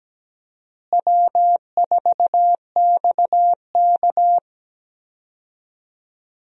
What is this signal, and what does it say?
Morse code, heard as a keyed tone, about 17 wpm, 700 Hz. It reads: W4XK
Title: USA Call Signs - 4 Characters - Repeated 3x - 17wpm